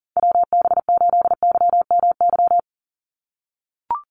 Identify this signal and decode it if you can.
Morse code, heard as a keyed tone, about 38 wpm, 700 Hz. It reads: WB8YMY
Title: USA Call Signs 40wpm